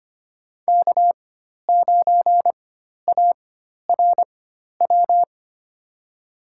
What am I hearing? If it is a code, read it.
Morse code, heard as a keyed tone, about 25 wpm, spaced as for 12 wpm, 700 Hz. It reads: K9ARW
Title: USA Call Signs - 4x Character Spacing - 25wpm